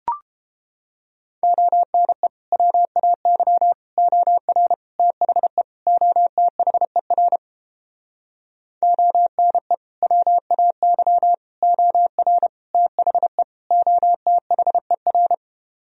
Morse code, heard as a keyed tone, about 33 wpm, 700 Hz. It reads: ONE WAY OR THE OTHER ONE WAY OR THE OTHER